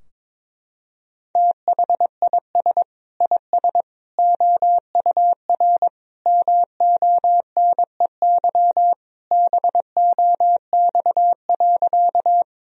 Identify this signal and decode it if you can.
Morse code, heard as a keyed tone, about 22 wpm, 700 Hz. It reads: THIS IS OUR MONEY BOX.
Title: Sentences from Top 1000 Words - Review Part 2 - 22wpm